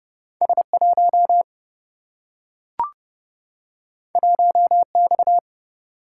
Morse code, heard as a keyed tone, about 30 wpm, 700 Hz. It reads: S 1 1 X
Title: Single-Letter Single-Number 30wpm